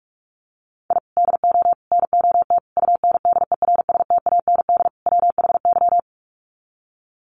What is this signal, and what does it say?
Morse code, heard as a keyed tone, about 44 wpm, 700 Hz. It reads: I DO NOT UNDERSTAND WHY
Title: Sets of 5 Words Encore 45wpm